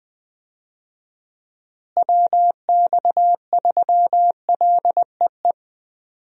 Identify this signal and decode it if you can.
Morse code, heard as a keyed tone, about 20 wpm, 700 Hz. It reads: WX3LEE
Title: USA Call Signs - 6 Characters - Repeated 2x - 20wpm